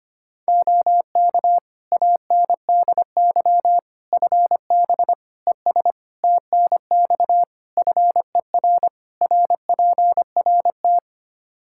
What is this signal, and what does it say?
Morse code, heard as a keyed tone, about 25 wpm, 700 Hz. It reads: OK ANDY FB ES TNX FER RPRT
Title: K1USN SST QSO Practice 25wpm